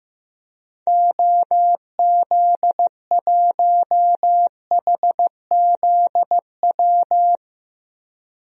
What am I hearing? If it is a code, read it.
Morse code, heard as a keyed tone, about 15 wpm, 700 Hz. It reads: OZ1HZW
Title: DXCC-100 Call Signs - 6 Characters - 15wpm